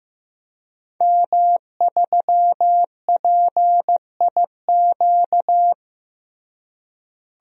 Morse code, heard as a keyed tone, about 15 wpm, 700 Hz. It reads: M3PIQ
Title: DXCC-100 Call Signs - 5 Characters - Repeated 2x - 15wpm